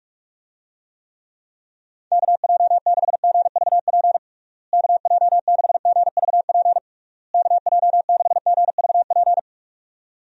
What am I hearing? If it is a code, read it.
Morse code, heard as a keyed tone, about 46 wpm, 700 Hz. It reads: KJ6GVP KJ6GVP KJ6GVP
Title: USA Call Signs - 6 Characters - Repeated 3x - 45wpm